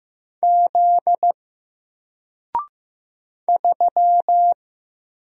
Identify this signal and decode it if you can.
Morse code, heard as a keyed tone, about 15 wpm, 700 Hz. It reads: Z 3
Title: Single Letter-Number 15wpm